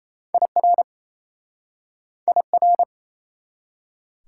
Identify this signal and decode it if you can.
Morse code, heard as a keyed tone, about 34 wpm, 700 Hz. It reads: IR IR
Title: Speed-Racing Prefixes and Suffixes 28wpm